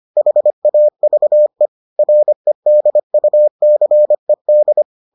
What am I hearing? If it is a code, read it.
Morse code, heard as a keyed tone, about 25 wpm, 600 Hz. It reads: HAVE REDUCED